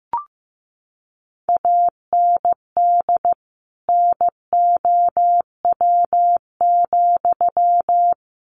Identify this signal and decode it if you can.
Morse code, heard as a keyed tone, about 15 wpm, 700 Hz. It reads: AND NOW,
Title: Sentences from Top 300 Words 15wpm